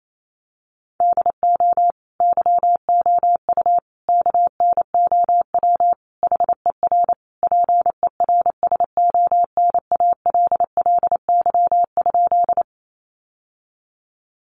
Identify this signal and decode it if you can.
Morse code, heard as a keyed tone, about 28 wpm, 700 Hz. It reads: DO YOU KNOW HER PERSONALLY?
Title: Sentences from Top 1400 Words 28wpm